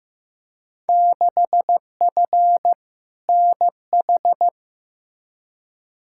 Morse code, heard as a keyed tone, about 15 wpm, 700 Hz. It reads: 6F NH
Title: Field Day Exchanges Encore - 15wpm